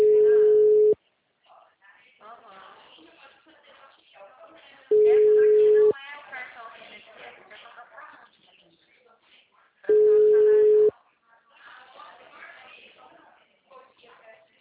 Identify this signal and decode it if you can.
Ringback tone (Europe, 425 Hz), with speech.